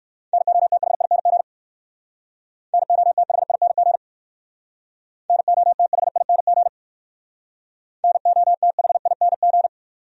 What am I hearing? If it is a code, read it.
Morse code, heard as a keyed tone, about 58 wpm, 700 Hz. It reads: NOTHING NOTHING NOTHING NOTHING